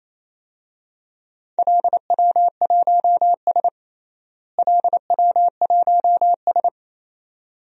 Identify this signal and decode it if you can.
Morse code, heard as a keyed tone, about 28 wpm, 700 Hz. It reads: LW1S LW1S
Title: DXCC-100 Call Signs - 4 Characters - Repeated 2x - 28wpm